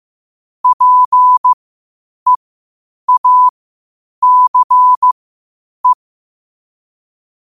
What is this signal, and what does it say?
Morse code, heard as a keyed tone, about 15 wpm, spaced as for 8 wpm, 1000 Hz. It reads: PEACE
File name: qotdfiles / 8wpm